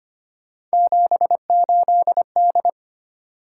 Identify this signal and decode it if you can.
Morse code, heard as a keyed tone, about 25 wpm, 700 Hz. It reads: 78D
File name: Three Letter-Number Combinations - 25wpm